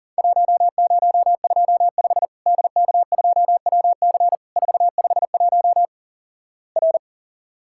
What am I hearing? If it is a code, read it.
Morse code, heard as a keyed tone, about 39 wpm, 700 Hz. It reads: 1025 DK2WC 451 R
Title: Worked All Europe DX Contest - QTC Exchange - 40wpm